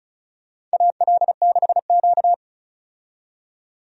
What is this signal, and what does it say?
Morse code, heard as a keyed tone, about 34 wpm, 700 Hz. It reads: AL6Q
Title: USA Call Signs - 4 Characters - 35wpm